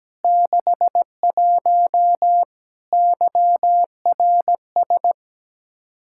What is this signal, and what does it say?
Morse code, heard as a keyed tone, about 17 wpm, 700 Hz. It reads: 61 YRS
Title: Practice QSO - Exchange 3 - More Info - 17wpm